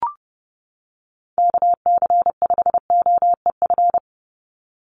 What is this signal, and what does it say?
Morse code, heard as a keyed tone, about 30 wpm, 700 Hz. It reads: KC5OEF
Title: USA Call Signs 30wpm